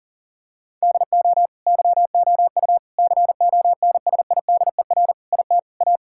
Morse code, heard as a keyed tone, about 40 wpm, 700 Hz. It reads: DO YOU CONSIDER IT A
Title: Sentences from Top 1000 Words - Review Part 6 - 40wpm